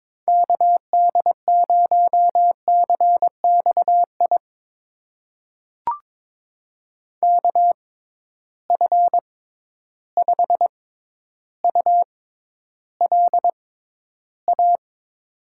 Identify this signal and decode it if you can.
Morse code, heard as a keyed tone, about 22 wpm, spaced as for 11 wpm, 700 Hz. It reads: KD0CXI KF5ULA
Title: USA Call Signs - 6 Characters - 6x Character Spacing - 22wpm